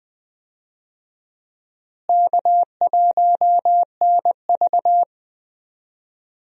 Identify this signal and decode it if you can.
Morse code, heard as a keyed tone, about 20 wpm, 700 Hz. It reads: K1NV
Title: USA Call Signs - 4 Characters - Repeated 3x - 20wpm